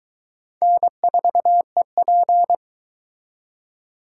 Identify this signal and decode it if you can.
Morse code, heard as a keyed tone, about 23 wpm, 700 Hz. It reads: N4EP